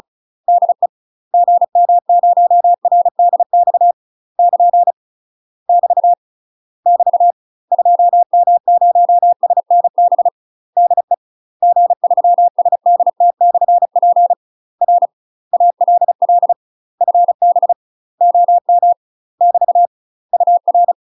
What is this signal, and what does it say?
Morse code, heard as a keyed tone, about 35 wpm, 700 Hz. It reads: DE GM0RDX <KN> <BT> <BT> 2M0SNB DE G3SDT/P R ALL FB OM <BT> UR
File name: UK MASTER QSOs TRAINING / 35WPM / UK MASTER QSOs 0032